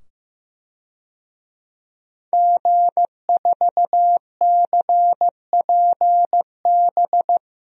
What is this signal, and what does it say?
Morse code, heard as a keyed tone, about 15 wpm, 700 Hz. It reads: G4CPB
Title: DXCC-100 Call Signs - 5 Characters - 15wpm